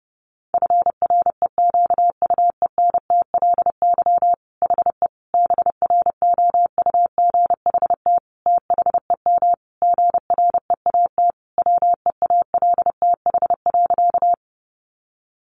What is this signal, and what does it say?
Morse code, heard as a keyed tone, about 30 wpm, 700 Hz. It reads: FREQUENTLY HE BROUGHT THEM GREAT WEALTH.
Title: Sentences from Top 1300 Words 30wpm